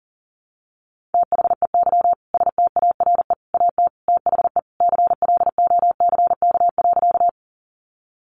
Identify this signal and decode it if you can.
Morse code, heard as a keyed tone, about 40 wpm, 700 Hz. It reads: THEY STARE AT THE CLOCK.
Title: Easy Phrases 40wpm